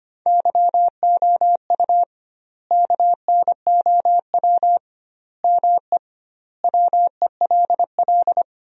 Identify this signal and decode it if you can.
Morse code, heard as a keyed tone, about 25 wpm, 700 Hz. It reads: YOU KNOW ME WELL
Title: 2x Word-Spacing - Sentences from Top 400 Words - 25wpm